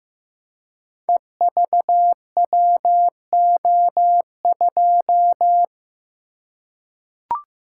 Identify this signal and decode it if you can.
Morse code, heard as a keyed tone, about 15 wpm, 700 Hz. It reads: EVWO2
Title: Code Groups - 5 Letter:Numbers - 15wpm